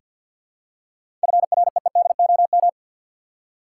Morse code, heard as a keyed tone, about 48 wpm, 700 Hz. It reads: FREEDOM
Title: Seven Letter Words - 50wpm